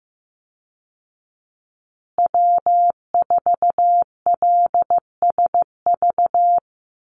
Morse code, heard as a keyed tone, about 14 wpm, 700 Hz. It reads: W4LSV
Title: Call Signs Repeated 2x - 15wpm